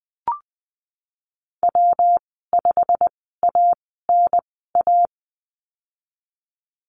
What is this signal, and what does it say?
Morse code, heard as a keyed tone, about 20 wpm, spaced as for 15 wpm, 700 Hz. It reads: W5ANA
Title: 147 Lesson 24g - Letter y - Call Signs 20wpm